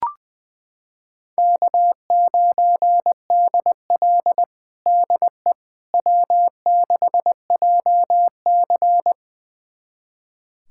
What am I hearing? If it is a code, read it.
Morse code, heard as a keyed tone, about 20 wpm, 700 Hz. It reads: K9DL DE W6JC
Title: USA Callsign DE Callsign - 4 Characters - 20wpm